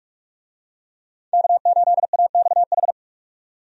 Morse code, heard as a keyed tone, about 47 wpm, 700 Hz. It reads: K8AXH
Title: USA Call Signs - 5 Characters - 45wpm